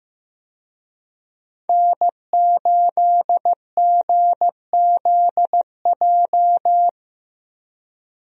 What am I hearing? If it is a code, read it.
Morse code, heard as a keyed tone, about 15 wpm, 700 Hz. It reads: N8GZJ
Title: USA Call Signs - 5 Characters - 15wpm